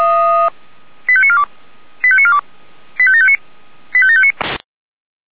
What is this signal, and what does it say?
Five-tone selective call (ZVEI), heard as quick runs of five tones, tones of about 70 ms. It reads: 85942 85942 85759 85759
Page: ZVEI